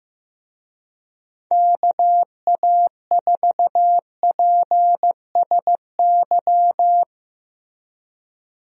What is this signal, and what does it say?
Morse code, heard as a keyed tone, about 15 wpm, 700 Hz. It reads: KA4PSY
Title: USA Call Signs - 6 Characters - Repeated 2x - 15wpm